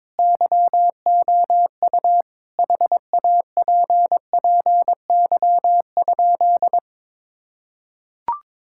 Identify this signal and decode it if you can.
Morse code, heard as a keyed tone, about 22 wpm, 700 Hz. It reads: YOU HAPPY?
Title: Sentences from Top 1000 Words - Review Part 6 - 22wpm